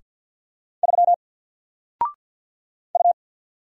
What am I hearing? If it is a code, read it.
Morse code, heard as a keyed tone, about 50 wpm, 700 Hz. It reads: 3 U
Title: Single Letter-Number 50wpm